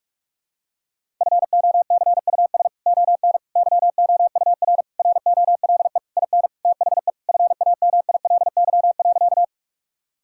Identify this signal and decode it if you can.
Morse code, heard as a keyed tone, about 46 wpm, 700 Hz. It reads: FOCUS ON YOUR ROLE IN THE FAMILY.
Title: Sentences from Top 1000 Words - Review Part 6 - 45wpm